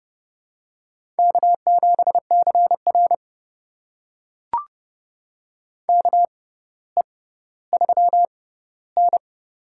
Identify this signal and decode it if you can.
Morse code, heard as a keyed tone, about 31 wpm, spaced as for 16 wpm, 700 Hz. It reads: K7CR KE3N
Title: USA Call Signs - 4 Characters - 6x Character Spacing - 30wpm